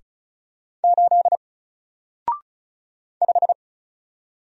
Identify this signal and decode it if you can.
Morse code, heard as a keyed tone, about 33 wpm, 700 Hz. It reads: 8 5
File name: Single-Letter-Number-35wpm